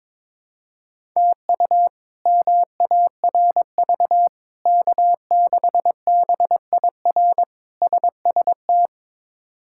Morse code, heard as a keyed tone, about 22 wpm, 700 Hz. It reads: TU MARV K6BIR SST
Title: K1USN SST QSO Practice - 22wpm